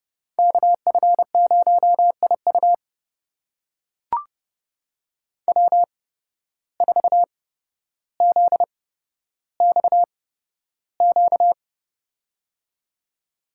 Morse code, heard as a keyed tone, about 30 wpm, spaced as for 13 wpm, 700 Hz. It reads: KF0IU W4ZXQ
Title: USA Call Signs - 5 Characters - 8x Character Spacing - 30wpm